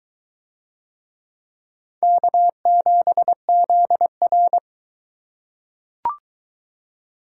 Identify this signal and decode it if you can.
Morse code, heard as a keyed tone, about 20 wpm, 700 Hz. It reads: K7ZR